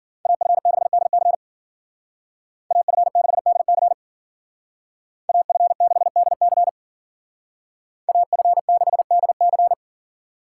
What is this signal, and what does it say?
Morse code, heard as a keyed tone, about 48 wpm, 700 Hz. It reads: AF6DC AF6DC AF6DC AF6DC
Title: Speed Racing - USA 2x2 Call Signs 40wpm